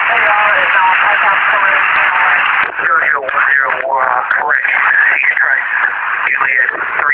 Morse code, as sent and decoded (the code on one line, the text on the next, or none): none